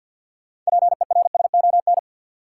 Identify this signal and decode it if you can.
Morse code, heard as a keyed tone, about 46 wpm, 700 Hz. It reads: PERSON
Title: Sentences from Top 400 Words - New Word Drill - 50wpm